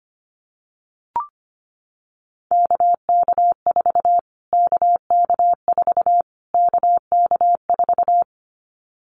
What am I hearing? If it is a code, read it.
Morse code, heard as a keyed tone, about 25 wpm, 700 Hz. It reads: KK4 KK4 KK4
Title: Callsign Prefixes for Top 25 DXCC - Repeated 3x - 25wpm